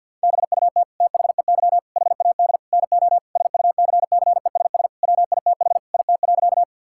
Morse code, heard as a keyed tone, about 50 wpm, 700 Hz. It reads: BUT THEY HAD NO SUCCESS WITH IT.
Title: Sentences from Top 1000 Words - Review Part 6 - 50wpm